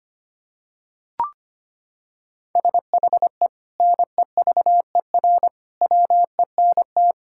Morse code, heard as a keyed tone, about 25 wpm, 700 Hz. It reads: SHE NEVER WENT
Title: Sentences from Top 1000 Words - Review Part 5 - 25wpm